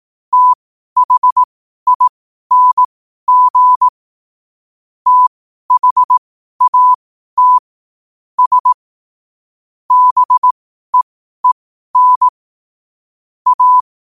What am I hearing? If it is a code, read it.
Morse code, heard as a keyed tone, about 18 wpm, spaced as for 12 wpm, 1000 Hz. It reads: THING THAT S BEEN A
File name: qotdfiles / 12wpm